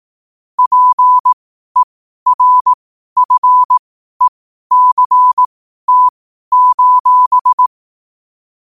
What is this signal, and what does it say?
Morse code, heard as a keyed tone, about 18 wpm, spaced as for 12 wpm, 1000 Hz. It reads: PERFECT:
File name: qotdfiles / 12wpm